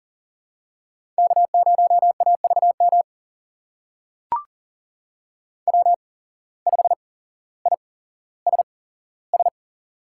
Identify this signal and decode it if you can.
Morse code, heard as a keyed tone, about 40 wpm, spaced as for 17 wpm, 700 Hz. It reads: K0AVM W5ISS
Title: USA Call Signs - 5 Characters - 8x Character Spacing - 40wpm